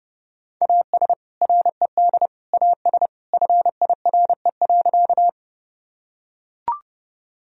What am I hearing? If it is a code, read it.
Morse code, heard as a keyed tone, about 30 wpm, 700 Hz. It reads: AS RED AS FIRE.
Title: Sentences from Top 1000 Words - Review Part 2 - 30wpm